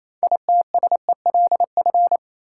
Morse code, heard as a keyed tone, about 28 wpm, 700 Hz. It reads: ITSELF
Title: Sentences from Top 800 Words - New Word Drill - 28wpm